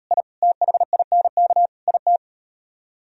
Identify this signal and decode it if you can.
Morse code, heard as a keyed tone, about 36 wpm, 690 Hz. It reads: I THINK IT